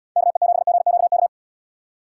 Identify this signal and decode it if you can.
Morse code, heard as a keyed tone, about 72 wpm, 700 Hz. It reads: KE7GQK